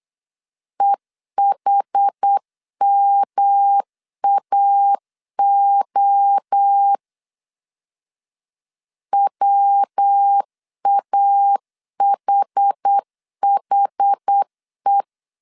Morse code, heard as a keyed tone, about 8 wpm, 800 Hz. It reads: EHMAO WAHHE
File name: CW / DET / 1 3 1500C S B 8wpm 4141